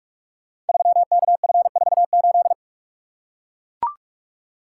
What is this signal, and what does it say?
Morse code, heard as a keyed tone, about 44 wpm, 700 Hz. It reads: 3KF48
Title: Code Groups - 5 Letter:Numbers - 45wpm